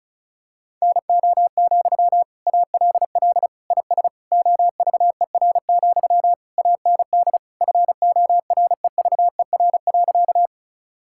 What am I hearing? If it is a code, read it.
Morse code, heard as a keyed tone, about 36 wpm, 700 Hz. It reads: NO, ALL IS OVER, AND FOREVER.
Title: Sentences from Top 1600 Words 35wpm